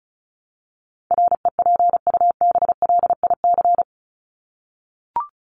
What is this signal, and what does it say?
Morse code, heard as a keyed tone, about 34 wpm, 700 Hz. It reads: REPUBLIC
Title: Top 1251-1500 Words 35wpm